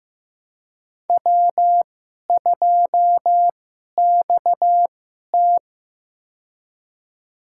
Morse code, heard as a keyed tone, about 15 wpm, spaced as for 11 wpm, 700 Hz. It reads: W2XT
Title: USA Call Signs - 4 Characters - 2x Character Spacing - 15wpm